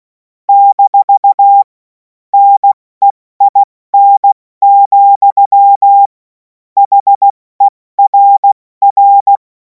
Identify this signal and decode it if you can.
Morse code, heard as a keyed tone, about 16 wpm, 800 Hz. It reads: - NEIN, HERR
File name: Morsekurs / Witze / Witz 07. Juli 2026 70 F80